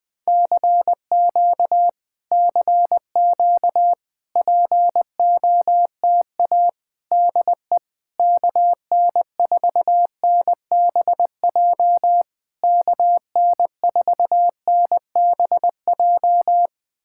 Morse code, heard as a keyed tone, about 20 wpm, 700 Hz. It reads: CQ CQ POTA DE KN4NBJ KN4NBJ
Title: POTA QSO Practice - 20wpm